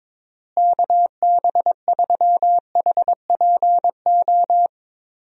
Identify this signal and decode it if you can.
Morse code, heard as a keyed tone, about 22 wpm, 700 Hz. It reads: KB3HPO
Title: USA Call Signs - 6 Characters - Repeated 3x - 22wpm